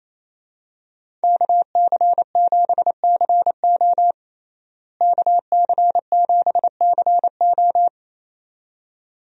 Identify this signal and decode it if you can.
Morse code, heard as a keyed tone, about 28 wpm, 700 Hz. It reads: KC7CO KC7CO
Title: USA Call Signs - 5 Characters - Repeated 2x - 28wpm